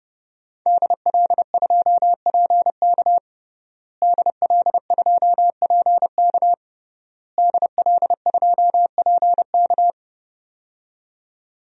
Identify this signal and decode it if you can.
Morse code, heard as a keyed tone, about 32 wpm, 700 Hz. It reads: DL2PK DL2PK DL2PK
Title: DXCC-100 Call Signs - 5 Characters - Repeated 3x - 30wpm